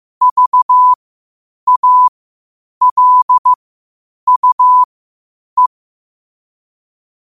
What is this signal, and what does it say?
Morse code, heard as a keyed tone, about 15 wpm, spaced as for 8 wpm, 1000 Hz. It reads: VALUE
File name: qotdfiles / 8wpm